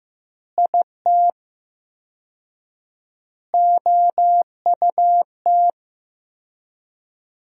Morse code, heard as a keyed tone, about 15 wpm, 700 Hz. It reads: IT OUT
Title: 4x Word-Spacing - Sentences from Top 200 Words - 15wpm